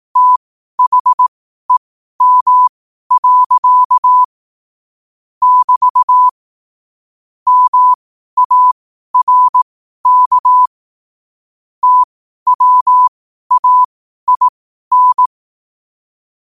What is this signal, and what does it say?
Morse code, heard as a keyed tone, about 18 wpm, spaced as for 12 wpm, 1000 Hz. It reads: THEM. <BT> MARK TWAIN
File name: qotdfiles / 12wpm